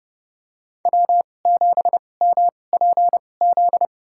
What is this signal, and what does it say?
Morse code, heard as a keyed tone, about 30 wpm, spaced as for 22 wpm, 700 Hz. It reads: W7MPZ
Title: USA Call Signs - 5 Characters - 2x Character Spacing - 30wpm